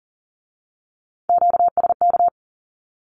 Morse code, heard as a keyed tone, about 41 wpm, 700 Hz. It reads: QSK
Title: Q-Codes with Spoken Meaning - 40wpm